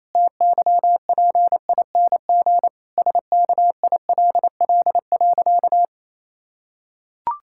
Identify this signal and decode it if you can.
Morse code, heard as a keyed tone, about 28 wpm, 700 Hz. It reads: TYPING SKILL.
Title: Pangram Challenge v2 - 28wpm